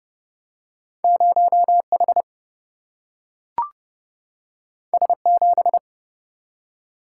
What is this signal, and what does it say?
Morse code, heard as a keyed tone, about 30 wpm, 700 Hz. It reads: 0H S7
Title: Single-Letter Single-Number 30wpm